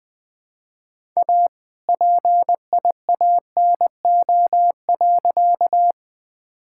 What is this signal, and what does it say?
Morse code, heard as a keyed tone, about 21 wpm, 700 Hz. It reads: A PIANO.
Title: All-Age Appropriate Jokes - 20wpm